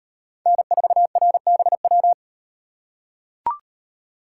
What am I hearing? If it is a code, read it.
Morse code, heard as a keyed tone, about 35 wpm, 700 Hz. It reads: N4RBW